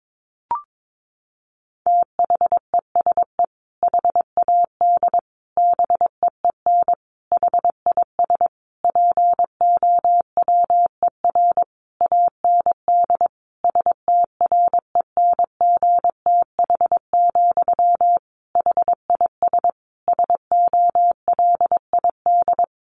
Morse code, heard as a keyed tone, about 22 wpm, 700 Hz. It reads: THESE HAD BEEN HIS POWER AND STRENGTH, HIS SOLID
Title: Sentences from Top 1500 Words 22wpm